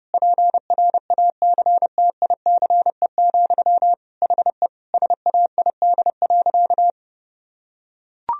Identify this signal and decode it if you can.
Morse code, heard as a keyed tone, about 30 wpm, 700 Hz. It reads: PRACTICE, HE SAID.
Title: Sentences from Top 1000 Words - Review Part 4 - 30wpm